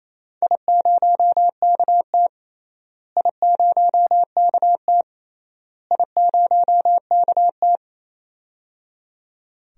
Morse code, heard as a keyed tone, about 28 wpm, 700 Hz. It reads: I0KT I0KT I0KT
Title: DXCC-100 Call Signs - 4 Characters - Repeated 3x - 28wpm